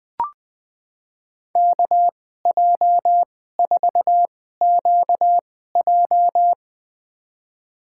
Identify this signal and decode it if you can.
Morse code, heard as a keyed tone, about 20 wpm, spaced as for 14 wpm, 700 Hz. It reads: KJ4QJ
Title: USA Call Signs - 2x Character Spacing - 20wpm